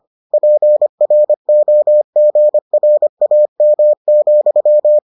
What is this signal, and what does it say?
Morse code, heard as a keyed tone, about 25 wpm, 600 Hz. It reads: PROGRAM,